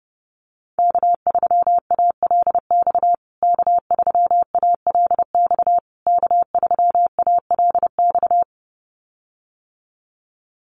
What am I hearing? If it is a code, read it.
Morse code, heard as a keyed tone, about 28 wpm, 700 Hz. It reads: K3ALX K3ALX K3ALX
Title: USA Call Signs Repeated 3x 30wpm